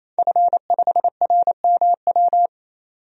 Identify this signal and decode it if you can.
Morse code, heard as a keyed tone, about 28 wpm, 700 Hz. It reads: F5RMW
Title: DXCC-100 Call Signs - 5 Characters - 28wpm